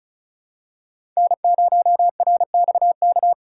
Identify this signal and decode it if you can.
Morse code, heard as a keyed tone, about 36 wpm, 700 Hz. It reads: N0RXK
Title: USA Call Signs - 5 Characters - 4x Character Spacing - 35wpm